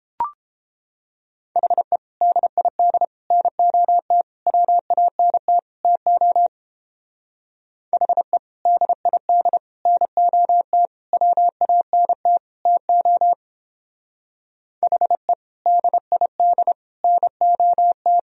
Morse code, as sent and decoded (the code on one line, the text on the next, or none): .... / -.. .. -.. / -. --- - / .-- .- -. - / - --- / .... / -.. .. -.. / -. --- - / .-- .- -. - / - --- / .... / -.. .. -.. / -. --- -
HE DID NOT WANT TO HE DID NOT WANT TO HE DID NOT